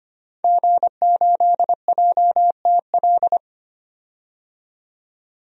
Morse code, heard as a keyed tone, about 25 wpm, 700 Hz. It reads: G8JTL
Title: DXCC-100 Call Signs - 5 Characters - 25wpm